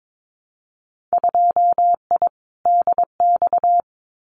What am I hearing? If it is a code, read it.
Morse code, heard as a keyed tone, about 22 wpm, 700 Hz. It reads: 2I DX
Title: Winter Field Day Exchanges 22wpm